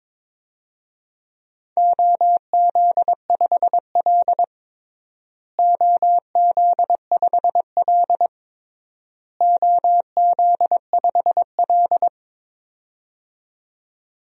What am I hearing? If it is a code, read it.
Morse code, heard as a keyed tone, about 22 wpm, 700 Hz. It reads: OZ5L OZ5L OZ5L
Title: DXCC-100 Call Signs - 4 Characters - Repeated 3x - 22wpm